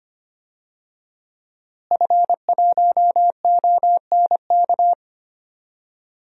Morse code, heard as a keyed tone, about 25 wpm, 700 Hz. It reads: F1ONK
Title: DXCC-100 Call Signs - 5 Characters - Repeated 3x - 25wpm